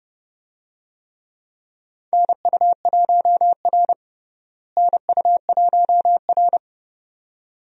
Morse code, heard as a keyed tone, about 30 wpm, 700 Hz. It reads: NU1R NU1R
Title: USA Call Signs - 4 Characters - Repeated 2x - 30wpm